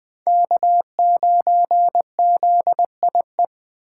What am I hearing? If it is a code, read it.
Morse code, heard as a keyed tone, about 21 wpm, 700 Hz. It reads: K9ZIE
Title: 2023 POTA Activator Call Signs - 20wpm